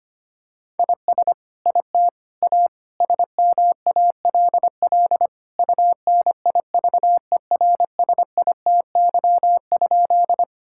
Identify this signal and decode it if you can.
Morse code, heard as a keyed tone, about 25 wpm, 700 Hz. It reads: IS IT A SMALL UNIVERSITY?
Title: Sentences from Top 1000 Words - Review Part 5 - 25wpm